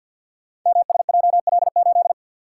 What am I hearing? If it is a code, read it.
Morse code, heard as a keyed tone, about 52 wpm, 700 Hz. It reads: MSJL8
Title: Code Groups - 5 Letter:Numbers - 50wpm